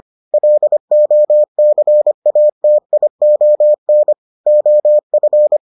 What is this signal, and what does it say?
Morse code, heard as a keyed tone, about 25 wpm, 600 Hz. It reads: LOCATION OF